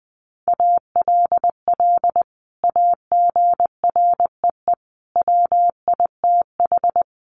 Morse code, heard as a keyed tone, about 20 wpm, 700 Hz. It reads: ALL AGREE WITH
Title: Sentences from Top 1100 Words 20wpm